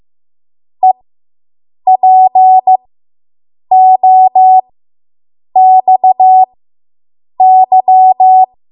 Morse code, heard as a keyed tone, about 15 wpm, spaced as for 7 wpm, 750 Hz. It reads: EPOXY